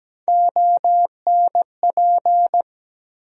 Morse code, heard as a keyed tone, about 17 wpm, 700 Hz. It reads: ONP
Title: USA Call Sign Postfixes - 3 Letters 17wpm